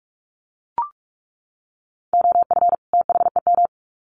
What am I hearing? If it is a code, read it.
Morse code, heard as a keyed tone, about 44 wpm, 700 Hz. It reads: OF THEM
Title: Sets of 2 Words 45wpm